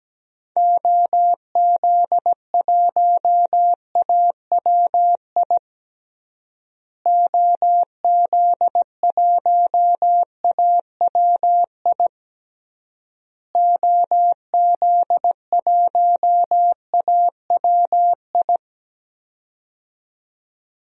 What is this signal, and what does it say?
Morse code, heard as a keyed tone, about 17 wpm, 700 Hz. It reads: OZ1AWI OZ1AWI OZ1AWI
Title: DXCC-100 Call Signs - 6 Characters - Repeated 3x - 17wpm